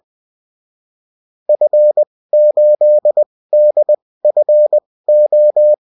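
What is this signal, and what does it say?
Morse code, heard as a keyed tone, about 20 wpm, spaced as for 16 wpm, 600 Hz. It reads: F8DFO
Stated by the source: Morse code bulletin of a French amateur CW union, sent as text and in CW.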